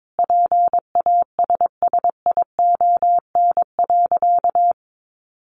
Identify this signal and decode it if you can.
Morse code, heard as a keyed tone, about 22 wpm, 700 Hz. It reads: PASSION.
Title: Sentences from Top 1300 Words 22wpm